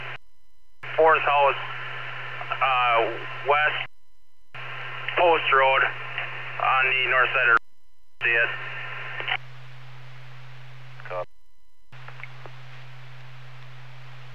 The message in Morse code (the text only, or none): none